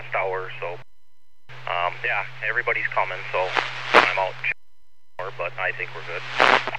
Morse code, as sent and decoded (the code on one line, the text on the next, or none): none